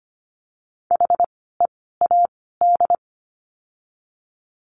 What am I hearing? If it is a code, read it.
Morse code, heard as a keyed tone, about 25 wpm, 700 Hz. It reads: H E A D
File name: course / 25:10wpm / 094 Lesson 16f - Letter c - Words 25wpm